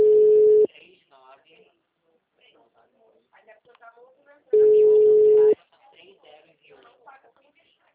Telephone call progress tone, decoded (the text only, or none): ringback tone